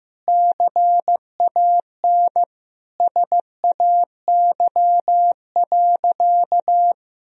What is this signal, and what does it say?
Morse code, heard as a keyed tone, about 15 wpm, 700 Hz. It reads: CAN SAY.
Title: Sentences from Top 1000 Words - Review Part 2 - 15wpm